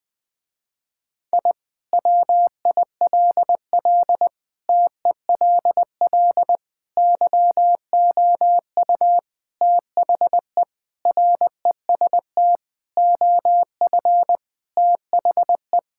Morse code, heard as a keyed tone, about 20 wpm, 700 Hz. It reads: I WILL TELL YOU THE REST OF THE
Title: Sentences with 4-Word Phrases - Spoken Beginning - 20wpm